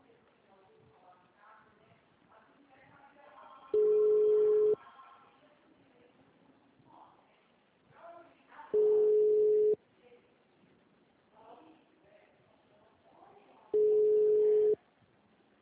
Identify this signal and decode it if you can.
Ringback tone (Europe, 425 Hz), with speech.